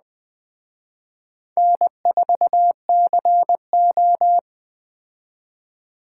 Morse code, heard as a keyed tone, about 20 wpm, 700 Hz. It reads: N4CO
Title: Speed Racing - USA Call Signs 17wpm